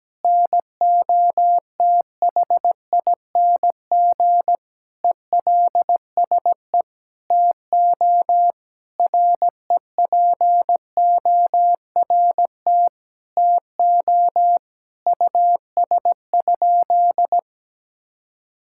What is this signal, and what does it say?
Morse code, heard as a keyed tone, about 17 wpm, 700 Hz. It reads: NOTHING ELSE TO REPORT TO US?
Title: Sentences from Top 1000 Words - Review Part 6 - 17wpm